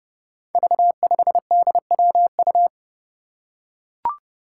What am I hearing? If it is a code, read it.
Morse code, heard as a keyed tone, about 30 wpm, 700 Hz. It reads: V5DWU
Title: Five Challenging Letter-Number Combinations 30wpm